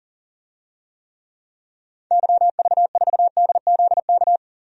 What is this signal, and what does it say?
Morse code, heard as a keyed tone, about 40 wpm, 700 Hz. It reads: YV4DZK
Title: DXCC-100 Call Signs - 6 Characters - 40wpm